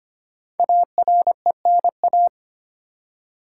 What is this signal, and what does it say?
Morse code, heard as a keyed tone, about 25 wpm, 700 Hz. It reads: ARENA
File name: 5 Letter Words - First 3 Letters Given - 25wpm